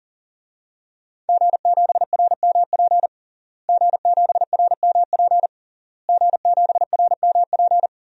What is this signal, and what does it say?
Morse code, heard as a keyed tone, about 40 wpm, 700 Hz. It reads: G7RMP G7RMP G7RMP
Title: DXCC-100 Call Signs - 5 Characters - Repeated 3x - 40wpm